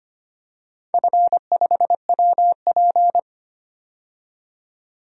Morse code, heard as a keyed tone, about 25 wpm, 700 Hz. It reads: F5WP
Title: DXCC-100 Call Signs - 4 Characters - 25wpm